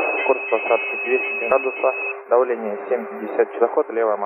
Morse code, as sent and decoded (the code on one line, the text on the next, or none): none